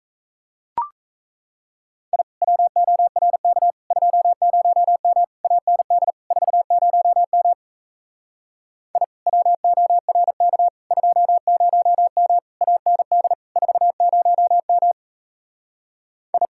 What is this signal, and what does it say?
Morse code, heard as a keyed tone, about 37 wpm, 700 Hz. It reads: I WORK 20M AND 40M I WORK 20M AND 40M I